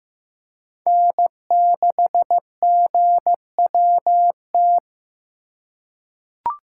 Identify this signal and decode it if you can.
Morse code, heard as a keyed tone, about 15 wpm, 700 Hz. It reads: N6GWT
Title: USA Call Signs - 5 Characters - 4x Character Spacing - 15wpm